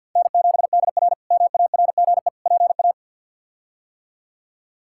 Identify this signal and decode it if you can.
Morse code, heard as a keyed tone, about 52 wpm, 700 Hz. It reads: N7DR MARGE PA
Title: K1USN SST QSO Practice - 50wpm